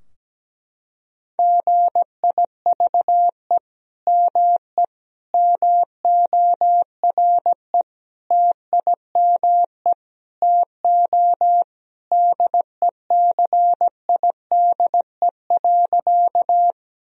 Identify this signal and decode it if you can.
Morse code, heard as a keyed tone, about 17 wpm, 700 Hz. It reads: GIVE ME MORE TIME TO DECIDE.
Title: Sentences from Top 1000 Words - Review Part 3 - 17wpm